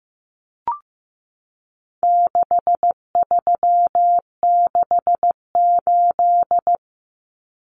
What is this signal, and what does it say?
Morse code, heard as a keyed tone, about 15 wpm, 700 Hz. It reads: 6368
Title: Common Cut Numbers - 4-Digit 15wpm